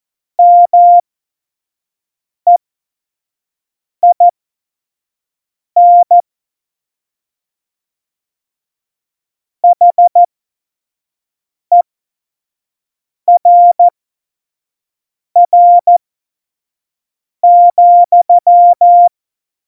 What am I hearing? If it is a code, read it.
Morse code, heard as a keyed tone, about 14 wpm, spaced as for 5 wpm, 700 Hz. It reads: MEIN HERR,
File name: Morsekurs / Witze / Witz 24.Dezember 2019, 25 F70